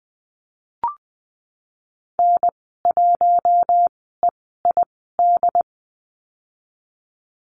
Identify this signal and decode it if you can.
Morse code, heard as a keyed tone, about 20 wpm, spaced as for 15 wpm, 700 Hz. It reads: N1EID
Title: 082 Lesson 14g - Number 2 - Call Signs 20wpm